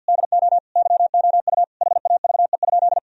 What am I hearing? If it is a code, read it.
Morse code, heard as a keyed tone, about 52 wpm, 700 Hz. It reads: DO YOU HAVE?
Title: Sentences from Top 1000 Words - Review Part 3 - 50wpm